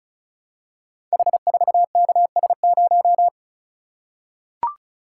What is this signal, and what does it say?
Morse code, heard as a keyed tone, about 36 wpm, 700 Hz. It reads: H4KS0
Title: Code Groups - 5 Letter:Numbers - 35wpm